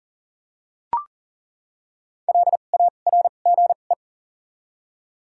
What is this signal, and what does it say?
Morse code, heard as a keyed tone, about 40 wpm, spaced as for 29 wpm, 700 Hz. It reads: LARGE
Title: 2x Character Spacing - Top 200 Words - 40wpm